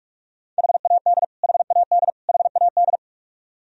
Morse code, heard as a keyed tone, about 41 wpm, 700 Hz. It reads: HAD HAD HAD
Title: Top 100 Words Repeated 45wpm